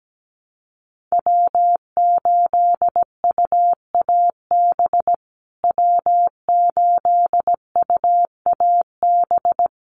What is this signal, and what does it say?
Morse code, heard as a keyed tone, about 17 wpm, 700 Hz. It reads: W8UAB W8UAB
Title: USA Call Signs Repeated 2x 17wpm